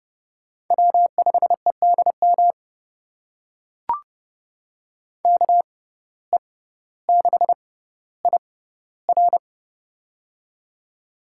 Morse code, heard as a keyed tone, about 30 wpm, spaced as for 15 wpm, 700 Hz. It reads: W5EDM KE6IR
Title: USA Call Signs - 5 Characters - 6x Character Spacing - 30wpm